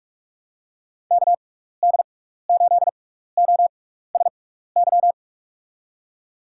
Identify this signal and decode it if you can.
Morse code, heard as a keyed tone, about 44 wpm, 700 Hz. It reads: K D 8 O S Y
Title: USA Call Signs - 6 Characters - 6x Character Spacing - 45wpm